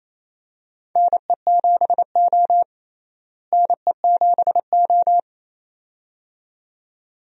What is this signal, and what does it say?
Morse code, heard as a keyed tone, about 28 wpm, 700 Hz. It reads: NE7O NE7O
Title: USA Call Signs - 4 Characters - Repeated 2x - 28wpm